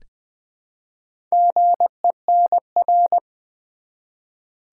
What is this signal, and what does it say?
Morse code, heard as a keyed tone, about 20 wpm, 700 Hz. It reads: GENR
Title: Build A Word - 1 Letter At A Time - 5-Letter Words - 20wpm